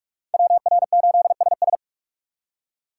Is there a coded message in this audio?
Yes: Morse code from 0.3 to 1.8 s, heard as a keyed tone, about 59 wpm, 700 Hz. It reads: WR8SS